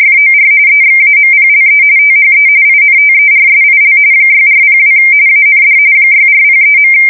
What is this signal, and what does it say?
RTTY, about 45.45 baud. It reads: FOR EXAMPLE THE INTERSIL 6402 AND 6403.